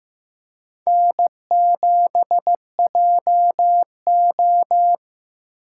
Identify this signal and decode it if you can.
Morse code, heard as a keyed tone, about 15 wpm, 700 Hz. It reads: N7JO
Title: USA Call Signs - 4 Characters - 4x Character Spacing - 15wpm